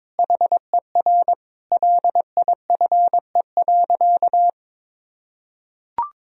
Morse code, heard as a keyed tone, about 22 wpm, 700 Hz. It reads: HER LIFE.
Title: Sentences from Top 1000 Words - Review Part 3 - 22wpm